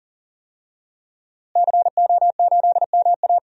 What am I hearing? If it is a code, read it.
Morse code, heard as a keyed tone, about 40 wpm, 700 Hz. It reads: CO8MA
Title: DXCC-100 Call Signs - 5 Characters - Repeated 2x - 40wpm